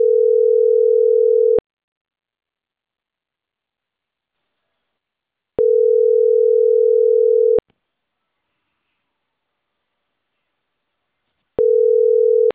Ringback tone (North America), 440 + 480 Hz.